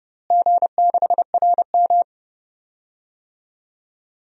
Morse code, heard as a keyed tone, about 31 wpm, 700 Hz. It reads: G6RM
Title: DXCC-100 Call Signs - 4 Characters - Repeated 3x - 30wpm